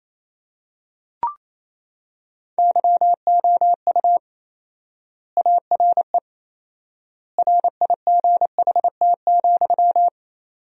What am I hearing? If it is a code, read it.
Morse code, heard as a keyed tone, about 28 wpm, 700 Hz. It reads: YOU ARE RIGHT,
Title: 4x Word-Spacing - Sentences from Top 300 Words - 28wpm